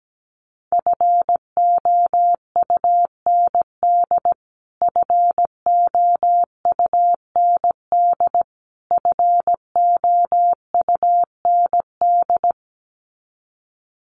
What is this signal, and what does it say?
Morse code, heard as a keyed tone, about 17 wpm, 700 Hz. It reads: FOUND FOUND FOUND
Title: Top 500 Words Repeated 17wpm